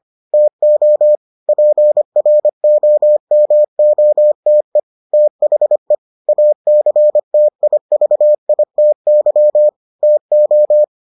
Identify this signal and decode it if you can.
Morse code, heard as a keyed tone, about 25 wpm, 600 Hz. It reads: TO PROMOTE THE ACTIVITY TO